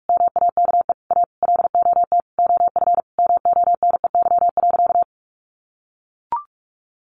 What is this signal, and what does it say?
Morse code, heard as a keyed tone, about 44 wpm, 700 Hz. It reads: MAKE A LOT OF MONEY.
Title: Easy Phrases 45wpm